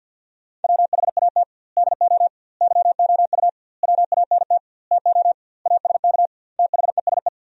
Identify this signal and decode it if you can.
Morse code, heard as a keyed tone, about 50 wpm, 700 Hz. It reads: WHAT DO YOU WANT TO ASK THESE
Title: 2x Word-Spacing - Sentences from Top 200 Words - 50wpm